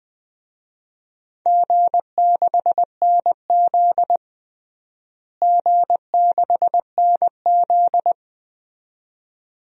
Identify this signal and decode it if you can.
Morse code, heard as a keyed tone, about 21 wpm, 700 Hz. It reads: G6NZ G6NZ
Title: DXCC-100 Call Signs - 4 Characters - Repeated 2x - 20wpm